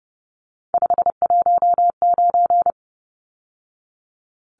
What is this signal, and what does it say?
Morse code, heard as a keyed tone, about 30 wpm, 700 Hz. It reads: 519
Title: Numbers - 3-Digit 30wpm